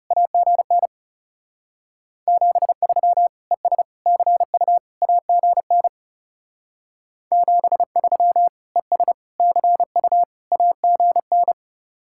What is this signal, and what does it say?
Morse code, heard as a keyed tone, about 38 wpm, 700 Hz. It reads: AGN 73 ES CU AGN 73 ES CU AGN